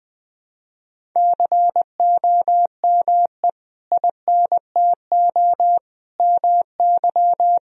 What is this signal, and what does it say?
Morse code, heard as a keyed tone, about 21 wpm, 700 Hz. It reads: COME INTO MY
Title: Sentences from Top 1000 Words - Review Part 1 - 20wpm